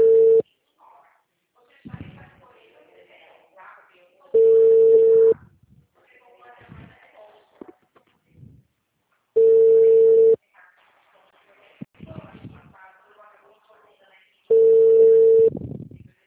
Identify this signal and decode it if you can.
Ringback tone (China, 450 Hz), with speech.
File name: out- -2003- - - .2784